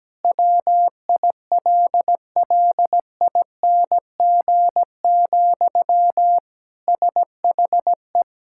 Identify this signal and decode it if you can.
Morse code, heard as a keyed tone, about 17 wpm, 700 Hz. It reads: WILLING, SHE
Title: Sentences from Top 1600 Words 17wpm